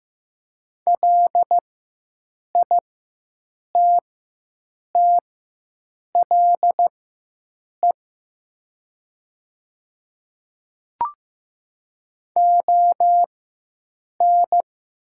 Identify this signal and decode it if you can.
Morse code, heard as a keyed tone, about 15 wpm, spaced as for 7 wpm, 700 Hz. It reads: LITTLE ON
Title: 4x Character Spacing - Top 200 Words - 15wpm